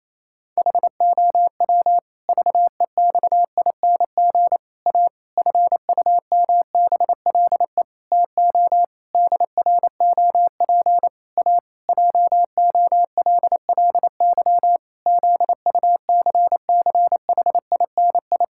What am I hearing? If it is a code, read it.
Morse code, heard as a keyed tone, about 28 wpm, 700 Hz. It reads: HOW VEXING A FUMBLE TO DROP A JOLLY ZUCCHINI
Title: Pangram Challenge v2 - 28wpm